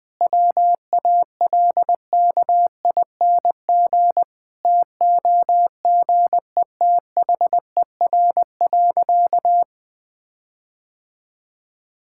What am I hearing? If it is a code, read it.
Morse code, heard as a keyed tone, about 20 wpm, 700 Hz. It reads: WALKING TOGETHER.
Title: Sentences with 5-Word Phrases - Spoken Beginning - 20wpm